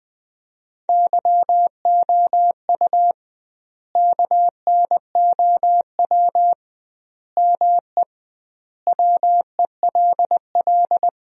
Morse code, heard as a keyed tone, about 21 wpm, 700 Hz. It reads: YOU KNOW ME WELL
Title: 2x Word-Spacing - Sentences from Top 400 Words - 20wpm